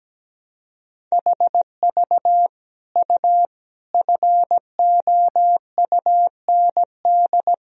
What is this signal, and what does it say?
Morse code, heard as a keyed tone, about 17 wpm, 700 Hz. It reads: HV U FOUND
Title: Sentences from Top 300 Words and Abbreviations 17wpm